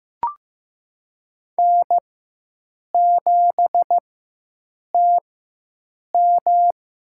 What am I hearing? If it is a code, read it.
Morse code, heard as a keyed tone, about 15 wpm, spaced as for 7 wpm, 700 Hz. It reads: N7TM
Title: USA Call Signs - 4 Characters - 4x Character Spacing - 15wpm